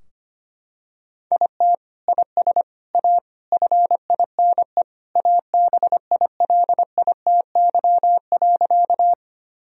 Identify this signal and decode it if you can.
Morse code, heard as a keyed tone, about 25 wpm, 700 Hz. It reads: IT IS A FINE ABILITY.
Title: Sentences from Top 1000 Words - Review Part 3 - 25wpm